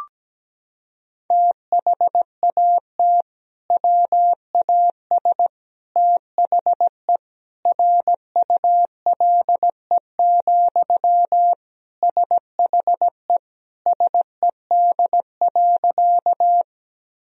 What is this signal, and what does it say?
Morse code, heard as a keyed tone, about 17 wpm, 700 Hz. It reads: THAT WAS THE RULE, SHE SED.
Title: Sentences from Top 400 Words and Abbreviations 17wpm